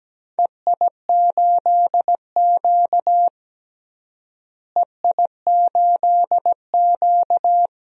Morse code, heard as a keyed tone, about 17 wpm, 700 Hz. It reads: EI8Q EI8Q
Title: DXCC-100 Call Signs - 4 Characters - Repeated 2x - 17wpm